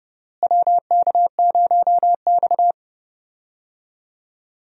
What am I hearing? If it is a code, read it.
Morse code, heard as a keyed tone, about 29 wpm, 700 Hz. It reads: WK0X
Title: USA Call Signs - 4 Characters - Repeated 2x - 30wpm